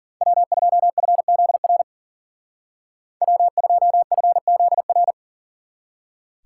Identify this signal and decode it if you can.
Morse code, heard as a keyed tone, about 52 wpm, 700 Hz. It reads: W2FZR W2FZR